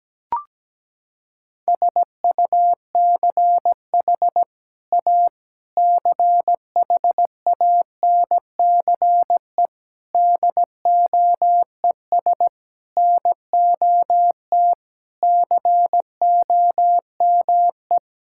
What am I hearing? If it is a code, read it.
Morse code, heard as a keyed tone, about 17 wpm, 700 Hz. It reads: SUCH A CHANCE DOES NOT COME
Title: Sentences from Top 1600 Words 17wpm